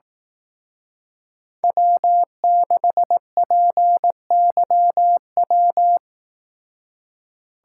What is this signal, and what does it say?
Morse code, heard as a keyed tone, about 18 wpm, 700 Hz. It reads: W6PYW
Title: Speed Racing - USA 1x3 Call Signs 15wpm